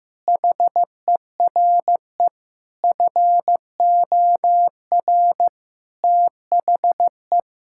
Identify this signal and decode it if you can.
Morse code, heard as a keyed tone, about 16 wpm, 700 Hz. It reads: HERE FOR THE
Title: Sentences from Top 1600 Words 15wpm